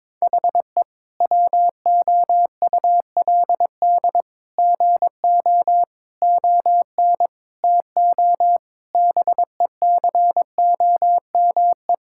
Morse code, heard as a keyed tone, about 22 wpm, 700 Hz. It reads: HE WOULD GO ON TO BECOME
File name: practice / 22wpm / Sentences with 8-Word Phrases - 22wpm